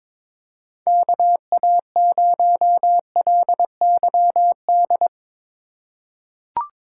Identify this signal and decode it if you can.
Morse code, heard as a keyed tone, about 22 wpm, 700 Hz. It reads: KA0LYD
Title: USA Call Signs - 2x Character Spacing - 22wpm